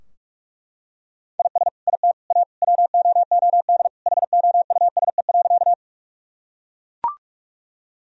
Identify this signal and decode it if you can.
Morse code, heard as a keyed tone, about 41 wpm, 700 Hz. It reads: IS IT A WOOD HOUSE.
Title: Sentences from Top 1000 Words - Review Part 2 - 45wpm